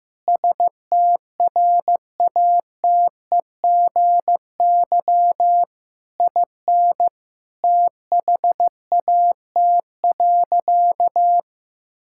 Morse code, heard as a keyed tone, about 15 wpm, 700 Hz. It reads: STRATEGY IN THAT.
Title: Sentences from Top 1000 Words - Review Part 5 - 15wpm